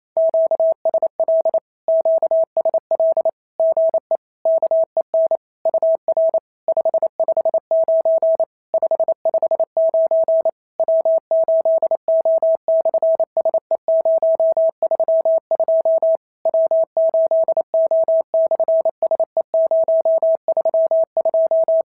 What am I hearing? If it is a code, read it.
Morse code, heard as a keyed tone, about 28 wpm, 690 Hz. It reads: QSL QSL GE KEN UR 559 559 W8O/SE032 W8O/SE032
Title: SOTA QSO Practice 28wpm